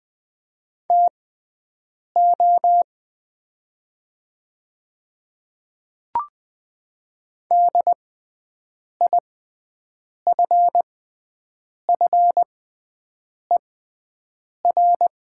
Morse code, heard as a keyed tone, about 20 wpm, spaced as for 7 wpm, 700 Hz. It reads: TO DIFFER
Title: 6x Character Spacing - Top 200 Words - 20wpm